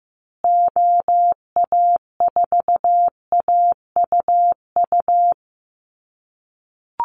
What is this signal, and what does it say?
Morse code, heard as a keyed tone, about 15 wpm, 700 Hz. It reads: OA4AUU
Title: USA Call Signs Repeated 2x 15wpm